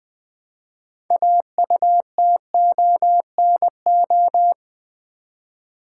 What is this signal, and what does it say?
Morse code, heard as a keyed tone, about 20 wpm, 700 Hz. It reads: AUTONO
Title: Build A Word - 2 Letters At A Time - 8-Letter Words - 20wpm